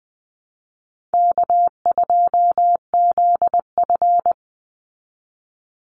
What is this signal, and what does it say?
Morse code, heard as a keyed tone, about 20 wpm, 700 Hz. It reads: K2ZF
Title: Easy Call Signs Repeated 3x - 20wpm